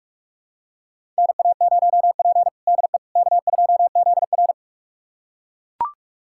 Morse code, heard as a keyed tone, about 41 wpm, 700 Hz. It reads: NA0P DE K2ZR
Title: USA Callsign DE Callsign - 4 Characters - 45wpm